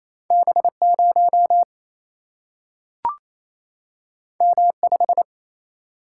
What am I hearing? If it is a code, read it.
Morse code, heard as a keyed tone, about 28 wpm, 700 Hz. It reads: B 0 M 5
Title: Single-Letter Single-Number 28wpm